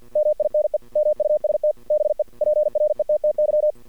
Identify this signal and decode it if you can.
Morse code, heard as a keyed tone, about 53 wpm, 610 Hz. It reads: MINE MUST BE PRETTY
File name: archived / CRSnet